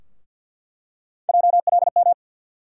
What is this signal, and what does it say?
Morse code, heard as a keyed tone, about 46 wpm, 700 Hz. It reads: JLM
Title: USA Call Sign Postfixes - 3 Letters 50wpm